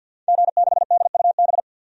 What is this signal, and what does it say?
Morse code, heard as a keyed tone, about 52 wpm, 700 Hz. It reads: G6DUB
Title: DXCC-100 Call Signs - 5 Characters - Repeated 2x - 50wpm